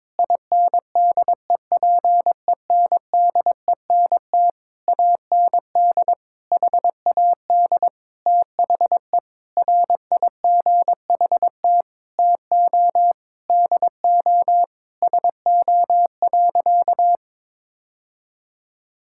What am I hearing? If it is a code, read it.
Morse code, heard as a keyed tone, about 22 wpm, 700 Hz. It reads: INDEPENDENT AND HAD THE RIGHT TO DO SO.
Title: Sentences from Top 1000 Words - Review Part 6 - 22wpm